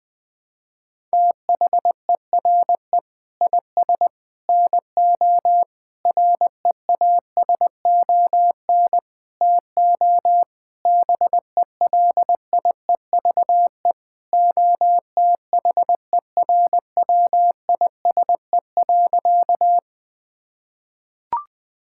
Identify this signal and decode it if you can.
Morse code, heard as a keyed tone, about 21 wpm, 700 Hz. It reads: THERE IS NO REASON TO BELIEVE OTHERWISE.
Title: Sentences with 6-Word Phrases - 20wpm